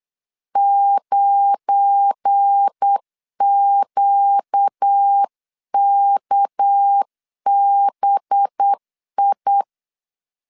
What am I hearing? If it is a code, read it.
Morse code, heard as a keyed tone, about 8 wpm, 800 Hz. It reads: 9QKBI
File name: CW / DET / 1 9 1500C S B 8wpm 5624